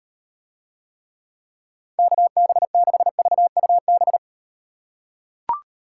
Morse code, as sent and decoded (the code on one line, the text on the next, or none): -.- -... -.... ...- ..- -...
KB6VUB